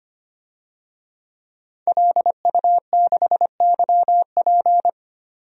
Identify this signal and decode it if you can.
Morse code, heard as a keyed tone, about 25 wpm, 700 Hz. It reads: LU6YP
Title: DXCC-100 Call Signs - 5 Characters - 25wpm